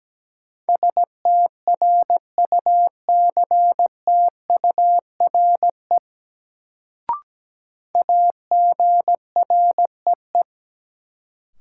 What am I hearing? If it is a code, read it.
Morse code, heard as a keyed tone, about 17 wpm, 700 Hz. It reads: STRUCTURE AGREE
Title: Sentences from Top 1000 Words - Review Part 6 - New Word Drill - 17wpm